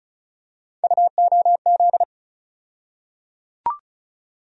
Morse code, heard as a keyed tone, about 36 wpm, 700 Hz. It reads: UOZ
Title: USA Call Sign Postfixes - 3 Letters 35wpm